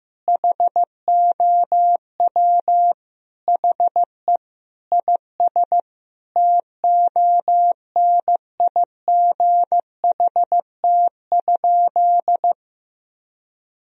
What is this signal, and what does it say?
Morse code, heard as a keyed tone, about 15 wpm, 700 Hz. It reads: HOW HE IS TONIGHT?
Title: Sentences from Top 1000 Words - Review Part 3 - 15wpm